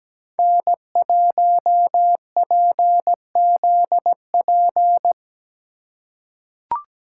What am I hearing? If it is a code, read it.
Morse code, heard as a keyed tone, about 17 wpm, 700 Hz. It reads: N1PZP
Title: USA Call Signs - 5 Characters - 2x Character Spacing - 17wpm